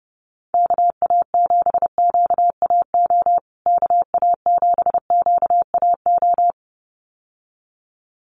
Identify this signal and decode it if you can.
Morse code, heard as a keyed tone, about 30 wpm, 700 Hz. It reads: KA7QAO KA7QAO
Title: USA Call Signs Repeated 2x 30wpm